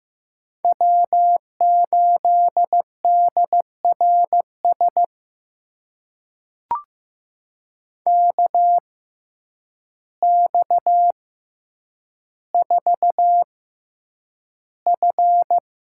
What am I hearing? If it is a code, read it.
Morse code, heard as a keyed tone, about 15 wpm, spaced as for 8 wpm, 700 Hz. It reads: W8DRS KX4F
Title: USA Call Signs - 5 Characters - 6x Character Spacing - 15wpm